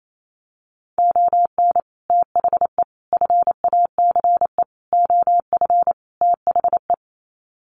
Morse code, heard as a keyed tone, about 28 wpm, 700 Hz. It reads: ON THE FACE OF THE
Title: Sets of 5 Words Encore2 28wpm